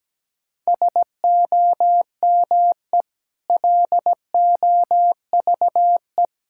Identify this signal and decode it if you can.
Morse code, heard as a keyed tone, about 17 wpm, 700 Hz. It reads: SOME LOVE
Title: Sentences from Top 1600 Words 17wpm